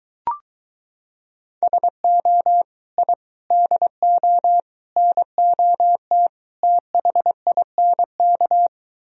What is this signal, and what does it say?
Morse code, heard as a keyed tone, about 23 wpm, 700 Hz. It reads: SO I DO NOT THINK